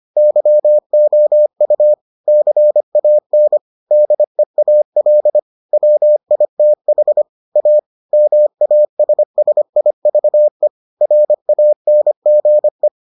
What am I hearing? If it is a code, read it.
Morse code, heard as a keyed tone, about 25 wpm, 600 Hz. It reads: YOU CAN DEAL WITH A MASSIVE RANGE